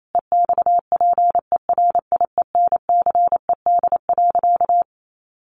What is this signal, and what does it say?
Morse code, heard as a keyed tone, about 28 wpm, 700 Hz. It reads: EXPERIENCED.
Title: Sentences from Top 1100 Words 28wpm